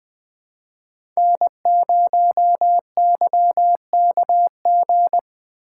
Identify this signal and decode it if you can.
Morse code, heard as a keyed tone, about 20 wpm, 700 Hz. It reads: N0YKG
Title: Speed Racing - USA 1x3 Call Signs 17wpm